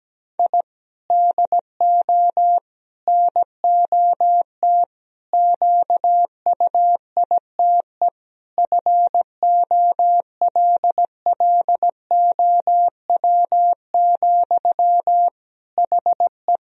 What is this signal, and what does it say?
Morse code, heard as a keyed tone, about 17 wpm, 700 Hz. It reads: I DO NOT QUITE FOLLOW, HE
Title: Sentences from Top 1000 Words - Review Part 1 - 17wpm